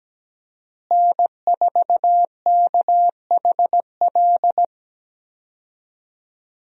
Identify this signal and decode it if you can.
Morse code, heard as a keyed tone, about 17 wpm, 700 Hz. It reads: N4KHL
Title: USA Call Signs - 5 Characters - Repeated 2x - 17wpm